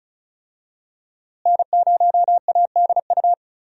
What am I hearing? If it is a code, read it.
Morse code, heard as a keyed tone, about 36 wpm, 700 Hz. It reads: N0ADU
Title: USA Call Signs - 5 Characters - Repeated 2x - 35wpm